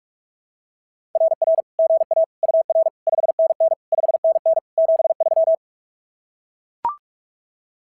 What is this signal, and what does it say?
Morse code, heard as a keyed tone, about 44 wpm, 660 Hz. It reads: RR GA UR 5NN 5NN 73
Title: SOTA QSO Practice 45wpm